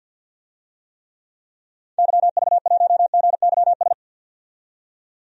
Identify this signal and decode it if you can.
Morse code, heard as a keyed tone, about 52 wpm, 700 Hz. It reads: YV1GYS
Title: DXCC-100 Call Signs - 6 Characters - 50wpm